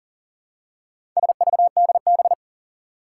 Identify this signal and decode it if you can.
Morse code, heard as a keyed tone, about 39 wpm, 700 Hz. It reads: SVDB
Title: Four Challenging Letter-Number Combinations 40wpm